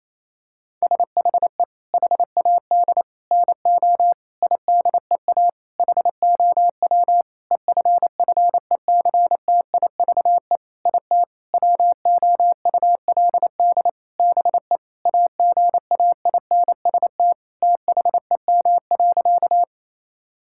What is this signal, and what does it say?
Morse code, heard as a keyed tone, about 28 wpm, 700 Hz. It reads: SHE HAD NO IDEA HOW EFFECTIVE IT WOULD BE AGAINST THEM.
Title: Sentences from Top 1000 Words - Review Part 6 - 28wpm